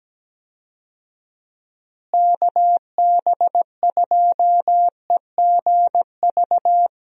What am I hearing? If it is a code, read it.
Morse code, heard as a keyed tone, about 17 wpm, 700 Hz. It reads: KB2EGV
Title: USA Call Signs - 6 Characters - Repeated 3x - 17wpm